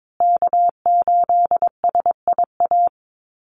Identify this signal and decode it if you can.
Morse code, heard as a keyed tone, about 22 wpm, 700 Hz. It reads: K8SIA
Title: Call Signs Repeated 3x - 22wpm